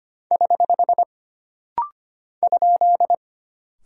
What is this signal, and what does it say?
Morse code, heard as a keyed tone, about 25 wpm, 700 Hz. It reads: <HH> ?
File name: Single Character (Letter:Number:Punctuation:Prosign) - Rapid-Fire - 25wpm